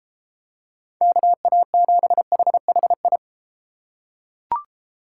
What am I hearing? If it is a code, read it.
Morse code, heard as a keyed tone, about 31 wpm, 700 Hz. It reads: KA7HHI